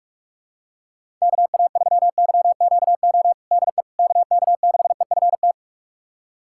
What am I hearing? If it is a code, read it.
Morse code, heard as a keyed tone, about 46 wpm, 700 Hz. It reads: KA3YQO DE KK6EFT
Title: USA Callsign DE Callsign - 6 Characters - 45wpm